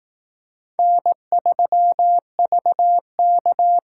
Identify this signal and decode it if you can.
Morse code, heard as a keyed tone, about 18 wpm, 700 Hz. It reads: N3VK
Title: Speed Racing - USA Call Signs 15wpm